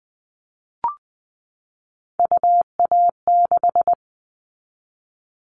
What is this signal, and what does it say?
Morse code, heard as a keyed tone, about 20 wpm, 700 Hz. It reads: UA6
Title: Callsign Prefixes for Top 100 DXCC - 20wpm